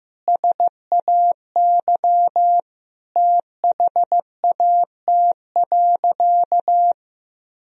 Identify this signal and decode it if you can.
Morse code, heard as a keyed tone, about 15 wpm, 700 Hz. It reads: SAY THAT.
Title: Sentences from Top 1000 Words - Review Part 1 - 15wpm